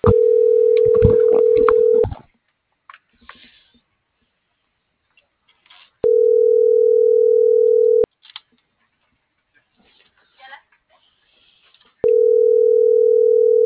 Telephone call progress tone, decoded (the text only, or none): ringback tone